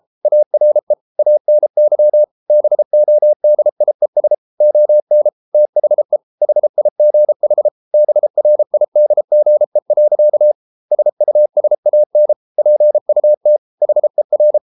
Morse code, heard as a keyed tone, about 33 wpm, 600 Hz. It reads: ARE ANY BODIES ON THE HIGH BRIDGE. SUSAN PUT HER